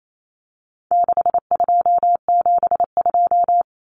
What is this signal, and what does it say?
Morse code, heard as a keyed tone, about 28 wpm, 700 Hz. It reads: 6272
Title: Numbers - 4-Digit 28wpm